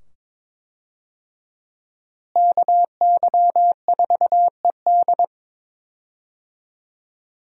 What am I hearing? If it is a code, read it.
Morse code, heard as a keyed tone, about 22 wpm, 700 Hz. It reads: KY4ED
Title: USA Call Signs - 5 Characters - Repeated 2x - 22wpm